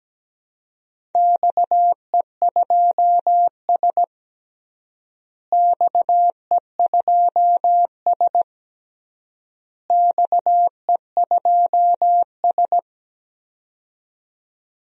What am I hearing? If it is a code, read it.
Morse code, heard as a keyed tone, about 17 wpm, 700 Hz. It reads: XE2S XE2S XE2S
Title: DXCC-100 Call Signs - 4 Characters - Repeated 3x - 17wpm